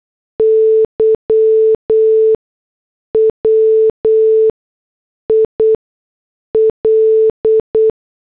Morse code, heard as a keyed tone, about 8 wpm, spaced as for 6 wpm, 440 Hz. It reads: Y WIL